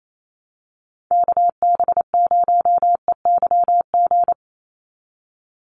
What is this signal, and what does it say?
Morse code, heard as a keyed tone, about 28 wpm, 700 Hz. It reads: KB0EYG
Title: USA Call Signs 28wpm